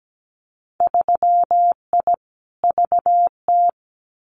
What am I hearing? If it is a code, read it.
Morse code, heard as a keyed tone, about 17 wpm, 700 Hz. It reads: 3I VT
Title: Winter Field Day Exchanges Encore 17wpm